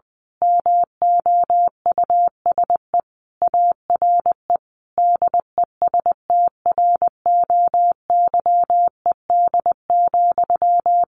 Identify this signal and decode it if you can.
Morse code, heard as a keyed tone, about 20 wpm, 700 Hz. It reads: MOUSE ARE DESTROYED,
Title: A Sound Of Thunder 20wpm